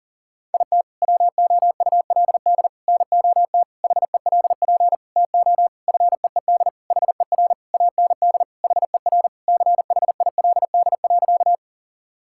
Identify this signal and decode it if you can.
Morse code, heard as a keyed tone, about 40 wpm, 700 Hz. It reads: IT WOULD NOT HELP TO FEED HER AND HER CHILD.
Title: Sentences from Top 1600 Words 40wpm